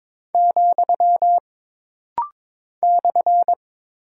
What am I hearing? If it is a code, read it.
Morse code, heard as a keyed tone, about 22 wpm, 700 Hz. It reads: , /
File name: practice / 22wpm / Single Punctuation-Prosign - Rapid-Fire - 22wpm